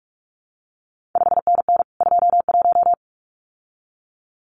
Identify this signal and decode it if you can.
Morse code, heard as a keyed tone, about 46 wpm, 700 Hz. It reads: 5NN 21
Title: CQ WW DX Contest Exchange Encore - 45wpm